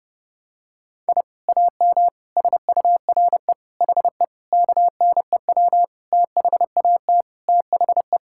Morse code, heard as a keyed tone, about 30 wpm, 700 Hz. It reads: I AM SURE HE KNEW THAT THE
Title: Sentences from Top 1000 Words - Review Part 6 - 30wpm